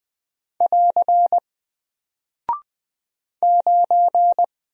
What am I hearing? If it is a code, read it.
Morse code, heard as a keyed tone, about 20 wpm, 700 Hz. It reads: <AR> 9
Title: Single Character (Letter:Number:Punctuation:Prosign) 20wpm